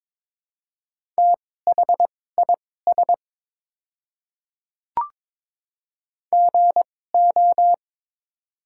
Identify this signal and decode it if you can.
Morse code, heard as a keyed tone, about 22 wpm, spaced as for 16 wpm, 700 Hz. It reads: THIS GO
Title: 2x Character Spacing - Top 100 Words - 22wpm